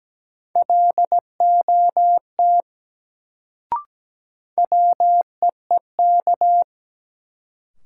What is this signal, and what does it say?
Morse code, heard as a keyed tone, about 17 wpm, 700 Hz. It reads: LOT WEEK
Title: Sentences from Top 500 Words - New Word Drill - 17wpm